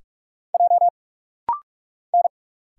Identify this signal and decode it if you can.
Morse code, heard as a keyed tone, about 46 wpm, 690 Hz. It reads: J N
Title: Single Letters - Rapid-Fire 45wpm